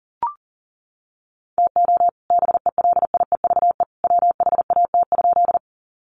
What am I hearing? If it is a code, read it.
Morse code, heard as a keyed tone, about 40 wpm, 700 Hz. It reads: TO BELIEVE WHAT?
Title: Sentences from Top 1000 Words 40wpm